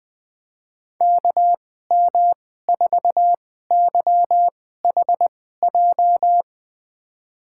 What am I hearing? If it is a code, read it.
Morse code, heard as a keyed tone, about 21 wpm, spaced as for 15 wpm, 700 Hz. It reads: KM4YHJ
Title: USA Call Signs - 6 Characters - 2x Character Spacing - 20wpm